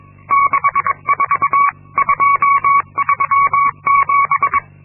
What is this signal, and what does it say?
Morse code, heard as a keyed tone, about 22 wpm, 1120 Hz. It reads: 64237
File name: Linksmakalnis-2